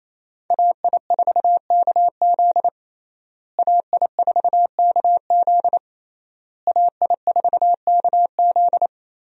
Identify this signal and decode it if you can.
Morse code, heard as a keyed tone, about 28 wpm, 700 Hz. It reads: AI4KZ AI4KZ AI4KZ
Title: USA Call Signs - 5 Characters - Repeated 3x - 28wpm